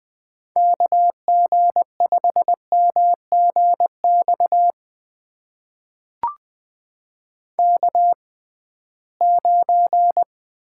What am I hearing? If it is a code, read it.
Morse code, heard as a keyed tone, about 21 wpm, spaced as for 15 wpm, 700 Hz. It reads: KG5MGX K9
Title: USA Call Signs - 6x Character Spacing - 20wpm